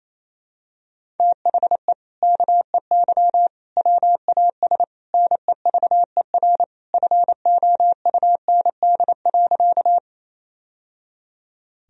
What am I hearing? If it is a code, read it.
Morse code, heard as a keyed tone, about 28 wpm, 700 Hz. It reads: THE KEY WAS NEVER FOUND.
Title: Sentences from Top 1000 Words - Review Part 4 - 28wpm